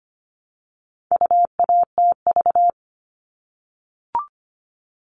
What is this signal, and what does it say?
Morse code, heard as a keyed tone, about 25 wpm, 700 Hz. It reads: UATV
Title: All Cut Numbers - 4-Digit 25wpm